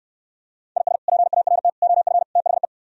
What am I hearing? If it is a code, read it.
Morse code, heard as a keyed tone, about 68 wpm, 700 Hz. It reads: IS PART OF THE